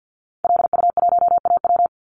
Morse code, heard as a keyed tone, about 50 wpm, 700 Hz. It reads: LU1AW
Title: Call Signs Repeated 2x - 50wpm